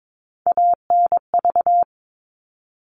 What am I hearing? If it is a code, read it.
Morse code, heard as a keyed tone, about 22 wpm, 700 Hz. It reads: ANV
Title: All Cut Numbers - 3-Digit 22wpm